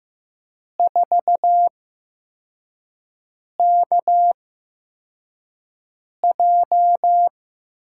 Morse code, heard as a keyed tone, about 15 wpm, spaced as for 4 wpm, 700 Hz. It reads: 4KJ
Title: USA Call Signs - 8x Character Spacing - 15wpm